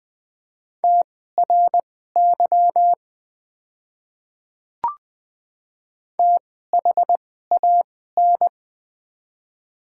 Morse code, heard as a keyed tone, about 20 wpm, spaced as for 14 wpm, 700 Hz. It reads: TRY THAN
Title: 2x Character Spacing - Top 200 Words - 20wpm